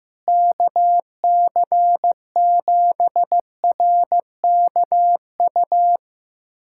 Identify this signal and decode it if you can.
Morse code, heard as a keyed tone, about 15 wpm, 700 Hz. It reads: KC7RKU
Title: USA Callsign DE Callsign 15wpm